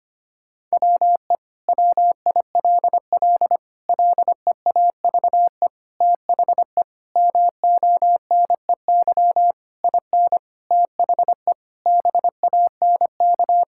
Morse code, heard as a keyed tone, about 25 wpm, 700 Hz. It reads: WE WILL LEAVE THE MONEY IN THE BANK
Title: Sentences from Top 1000 Words - Review Part 6 - 25wpm